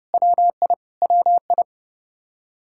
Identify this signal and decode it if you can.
Morse code, heard as a keyed tone, about 30 wpm, 700 Hz. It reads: WI WI
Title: K1USN SST QSO Practice 30wpm